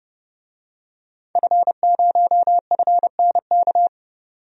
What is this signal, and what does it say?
Morse code, heard as a keyed tone, about 30 wpm, 700 Hz. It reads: F0FNK
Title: DXCC-100 Call Signs - 5 Characters - 30wpm